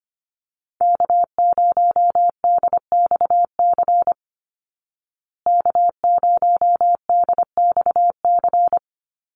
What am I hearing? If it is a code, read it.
Morse code, heard as a keyed tone, about 25 wpm, 700 Hz. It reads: K0DXC K0DXC
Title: Call Signs Repeated 3x - 25wpm